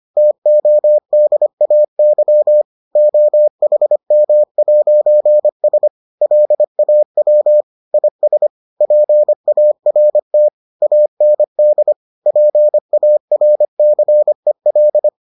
Morse code, heard as a keyed tone, about 25 wpm, 600 Hz. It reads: TODAY OHM'S LAW IS PART AND PARCEL